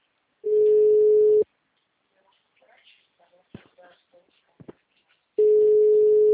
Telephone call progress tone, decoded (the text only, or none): ringback tone